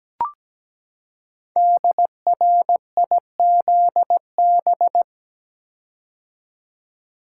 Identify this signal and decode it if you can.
Morse code, heard as a keyed tone, about 17 wpm, 700 Hz. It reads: DRIZB
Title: Code Groups - 5 Letters - 17wpm